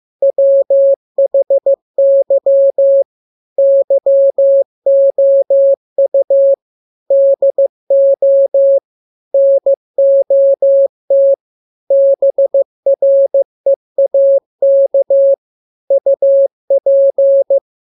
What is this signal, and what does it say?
Morse code, heard as a keyed tone, about 15 wpm, 550 Hz. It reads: WHY YOU DO NOT BREAK UP
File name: jokes / joke 18 15 0000